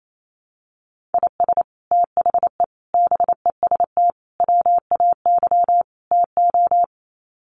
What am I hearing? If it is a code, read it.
Morse code, heard as a keyed tone, about 28 wpm, 700 Hz. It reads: IS THE BEST WAY TO
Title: Sets of 5 Words Encore 28wpm